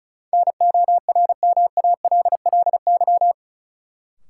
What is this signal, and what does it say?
Morse code, heard as a keyed tone, about 36 wpm, 700 Hz. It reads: NORMALLY
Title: Sentences from Top 1000 Words - Review Part 6 - New Word Drill - 35wpm